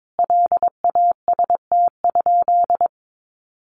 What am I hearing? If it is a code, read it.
Morse code, heard as a keyed tone, about 22 wpm, 700 Hz. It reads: LAST?
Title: Sentences from Top 1300 Words 22wpm